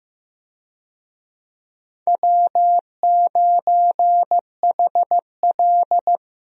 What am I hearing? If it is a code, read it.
Morse code, heard as a keyed tone, about 15 wpm, 700 Hz. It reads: W9HL
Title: USA Call Signs - 4 Characters - Repeated 2x - 15wpm